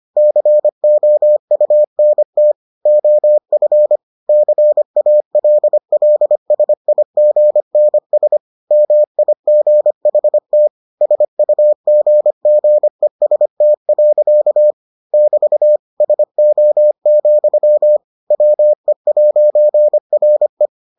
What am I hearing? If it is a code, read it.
Morse code, heard as a keyed tone, about 25 wpm, 600 Hz. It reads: COUNT OF CALLSIGNS MIGHT SUGGEST. <BT> SO, WE'RE